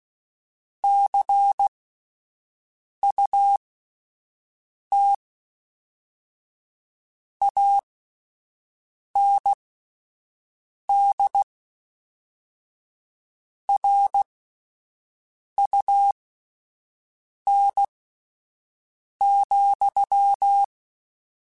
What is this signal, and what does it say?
Morse code, heard as a keyed tone, about 16 wpm, spaced as for 5 wpm, 800 Hz. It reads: CUT AND RUN,
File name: Misc / Misc-mp3s / Word Pairs @5